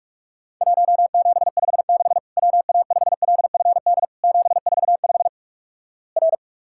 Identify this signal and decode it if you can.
Morse code, heard as a keyed tone, about 46 wpm, 700 Hz. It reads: 1756 WA5LFD 745 R
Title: Worked All Europe DX Contest - QTC Exchange - 45wpm